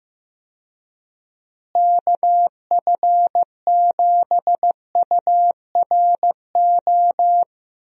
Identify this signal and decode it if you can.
Morse code, heard as a keyed tone, about 15 wpm, 700 Hz. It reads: KF7URO
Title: USA Call Signs - 6 Characters - Repeated 2x - 15wpm